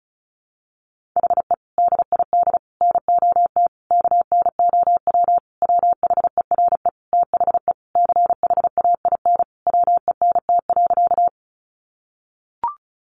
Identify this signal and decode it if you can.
Morse code, heard as a keyed tone, about 34 wpm, 700 Hz. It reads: HE DID NOT KNOW WHERE THE CHAIN WENT.
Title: Sentences from Top 1500 Words 35wpm